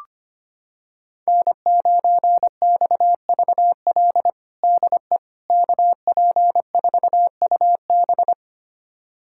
Morse code, heard as a keyed tone, about 25 wpm, 700 Hz. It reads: N9XVL DE KP4UB
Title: USA Callsign DE Callsign 25wpm